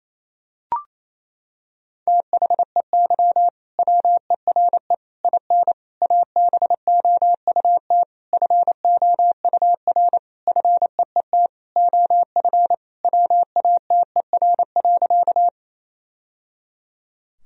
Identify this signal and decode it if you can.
Morse code, heard as a keyed tone, about 27 wpm, 700 Hz. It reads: THEY WERE IN ABOUT FOUR FEET OF WATER.
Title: Sentences from Top 1000 Words - Review Part 1 - 28wpm